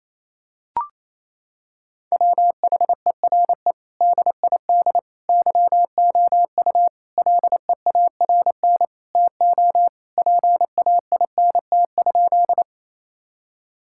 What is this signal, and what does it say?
Morse code, heard as a keyed tone, about 28 wpm, 700 Hz. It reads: WHERE DID YOU LEARN TO PAINT?
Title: Sentences from Top 1000 Words - Review Part 6 - 28wpm